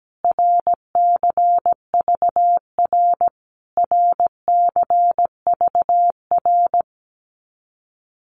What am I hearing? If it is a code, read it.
Morse code, heard as a keyed tone, about 17 wpm, 700 Hz. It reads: RCVR RCVR
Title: Abbreviations Repeated 3x - 17wpm